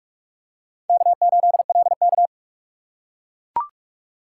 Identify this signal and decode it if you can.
Morse code, heard as a keyed tone, about 41 wpm, 700 Hz. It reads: K8LK
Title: USA Call Signs - 4 Characters - 4x Character Spacing - 45wpm